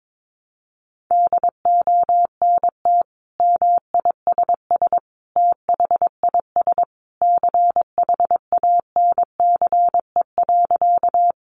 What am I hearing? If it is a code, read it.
Morse code, heard as a keyed tone, about 22 wpm, 700 Hz. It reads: DONT MISS THIS CHANCE.
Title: Sentences from Top 800 Words 22wpm